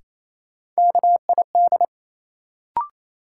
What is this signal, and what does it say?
Morse code, heard as a keyed tone, about 28 wpm, 700 Hz. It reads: KID
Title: Sentences from Top 1600 Words - New Word Drill Alt - 28wpm